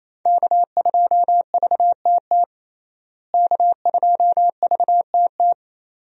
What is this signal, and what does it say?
Morse code, heard as a keyed tone, about 28 wpm, 700 Hz. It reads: K2VTT K2VTT
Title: USA Call Signs - 5 Characters - Repeated 2x - 28wpm